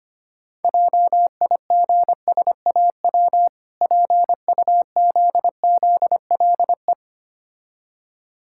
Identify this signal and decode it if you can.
Morse code, heard as a keyed tone, about 25 wpm, 700 Hz. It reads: JIGSAW PUZZLE
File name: practice / 25wpm / Difficult Words with Hints - 25wpm